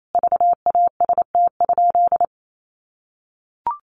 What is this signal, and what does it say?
Morse code, heard as a keyed tone, about 28 wpm, 700 Hz. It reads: VAST?
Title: Sentences from Top 1300 Words 28wpm